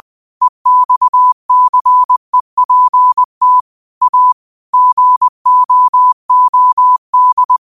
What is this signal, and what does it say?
Morse code, heard as a keyed tone, about 20 wpm, 1000 Hz. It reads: EXCEPT A GOOD
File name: qotdfiles / 20wpm